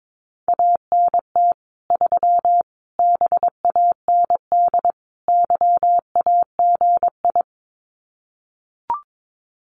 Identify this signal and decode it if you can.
Morse code, heard as a keyed tone, about 22 wpm, 700 Hz. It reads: ANT 3 BAND YAGI
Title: Common QSO Phrases 22wpm